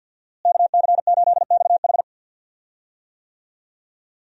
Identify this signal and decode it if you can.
Morse code, heard as a keyed tone, about 51 wpm, 700 Hz. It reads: KC8XH
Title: USA Call Signs - 5 Characters - Repeated 3x - 50wpm